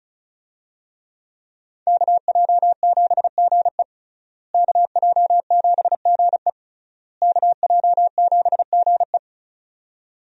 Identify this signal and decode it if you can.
Morse code, heard as a keyed tone, about 35 wpm, 700 Hz. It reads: KJ7GE KJ7GE KJ7GE
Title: USA Call Signs - 5 Characters - Repeated 3x - 35wpm